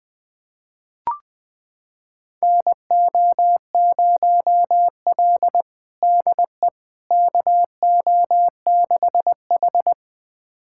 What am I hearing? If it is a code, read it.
Morse code, heard as a keyed tone, about 21 wpm, 700 Hz. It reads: NO0L DE KO6H
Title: USA Callsign DE Callsign - 4 Characters - 20wpm